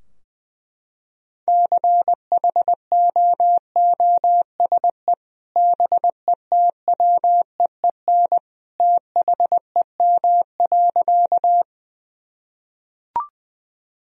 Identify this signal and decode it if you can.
Morse code, heard as a keyed tone, about 20 wpm, 700 Hz. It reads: CHOOSE BETWEEN THEM.
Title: Sentences from Top 1000 Words - Review Part 4 - 20wpm